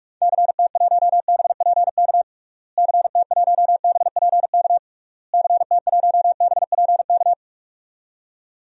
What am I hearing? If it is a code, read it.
Morse code, heard as a keyed tone, about 46 wpm, 700 Hz. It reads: CT1BPK CT1BPK CT1BPK
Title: DXCC-100 Call Signs - 6 Characters - Repeated 3x - 45wpm